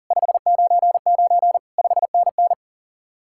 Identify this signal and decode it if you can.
Morse code, heard as a keyed tone, about 51 wpm, 700 Hz. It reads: 599 5NN